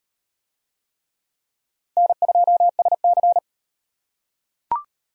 Morse code, heard as a keyed tone, about 34 wpm, 700 Hz. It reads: N2SC